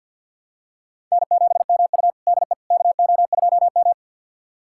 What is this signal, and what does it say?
Morse code, heard as a keyed tone, about 52 wpm, 700 Hz. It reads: N7MU DE KO2M
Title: USA Callsign DE Callsign - 4 Characters - 50wpm